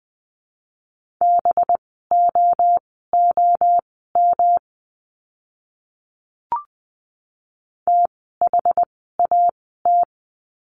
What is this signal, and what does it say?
Morse code, heard as a keyed tone, about 20 wpm, spaced as for 14 wpm, 700 Hz. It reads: BOOM THAT
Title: 185 Lesson 30f - Letter b - Words 20wpm